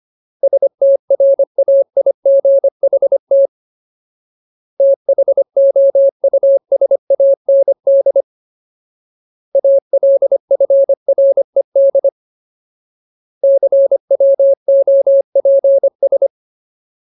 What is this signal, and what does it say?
Morse code, heard as a keyed tone, about 25 wpm, 550 Hz. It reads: STRAIGHT THOUSAND ALFRED CWOPS
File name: Ultimate Word List / 25WPM Ultimate